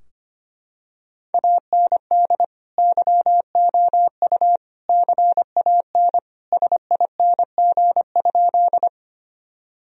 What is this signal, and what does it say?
Morse code, heard as a keyed tone, about 25 wpm, 700 Hz. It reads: AND YOU CAN SING?
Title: Sentences from Top 1000 Words - Review Part 2 - 25wpm